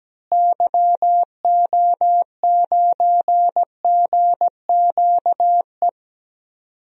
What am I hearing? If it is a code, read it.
Morse code, heard as a keyed tone, about 17 wpm, 700 Hz. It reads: YO9GQE
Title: DXCC-100 Call Signs - 6 Characters - Repeated 2x - 17wpm